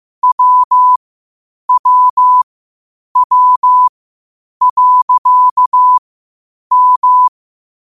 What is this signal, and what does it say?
Morse code, heard as a keyed tone, about 15 wpm, spaced as for 8 wpm, 1000 Hz. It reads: WWW.M
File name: qotdfiles / 8wpm